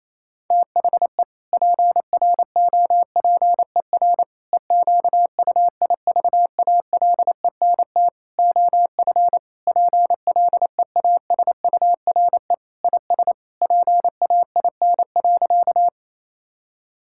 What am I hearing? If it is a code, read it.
Morse code, heard as a keyed tone, about 27 wpm, 700 Hz. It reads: THE PROPER EQUIVALENT OF PLEASURE IS PAIN.
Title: Sentences from Top 1600 Words 28wpm